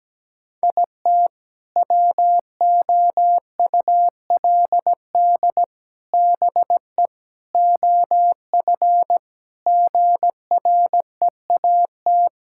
Morse code, heard as a keyed tone, about 17 wpm, 700 Hz. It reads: IT WOULD BE OF GREAT
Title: Sentences from Top 1000 Words - Review Part 6 - 17wpm